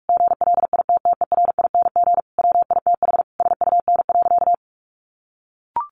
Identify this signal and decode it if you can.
Morse code, heard as a keyed tone, about 44 wpm, 700 Hz. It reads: GLITTERING WITH SUN.
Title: A Sound Of Thunder 45wpm